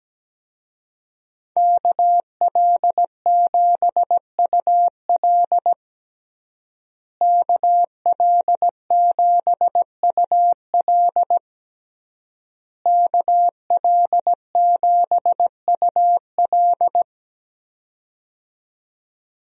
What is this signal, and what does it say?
Morse code, heard as a keyed tone, about 17 wpm, 700 Hz. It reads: KL7UL KL7UL KL7UL
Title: USA Call Signs - 5 Characters - Repeated 3x - 17wpm